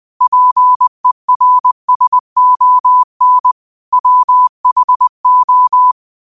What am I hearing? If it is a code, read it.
Morse code, heard as a keyed tone, about 20 wpm, 1000 Hz. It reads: PERSON WHO
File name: qotdfiles / 20wpm